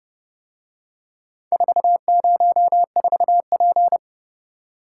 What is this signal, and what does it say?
Morse code, heard as a keyed tone, about 30 wpm, 700 Hz. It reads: 404P